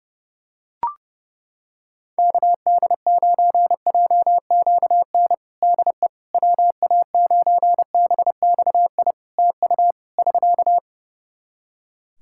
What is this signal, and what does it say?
Morse code, heard as a keyed tone, about 32 wpm, 700 Hz. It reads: KD9JQN DE WA9BXI TU <SK>
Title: Practice QSO - Exchange 4 - Ending - 30wpm